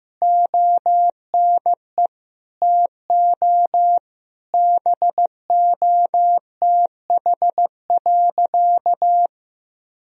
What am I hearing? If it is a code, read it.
Morse code, heard as a keyed tone, about 15 wpm, 700 Hz. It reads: ONE TO BOTH.
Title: Sentences from Top 1000 Words - Review Part 1 - 15wpm